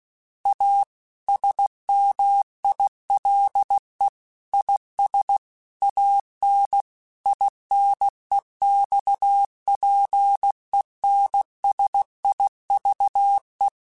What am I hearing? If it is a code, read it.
Morse code, heard as a keyed tone, about 16 wpm, 800 Hz. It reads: A SMILE IS AN INEXPENSIVE
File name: Quotes / Quotes-mp3s / A Smile @15